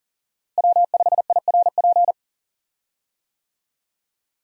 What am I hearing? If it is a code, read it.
Morse code, heard as a keyed tone, about 39 wpm, 700 Hz. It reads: W5IRP
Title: USA Call Signs - 5 Characters - Repeated 3x - 40wpm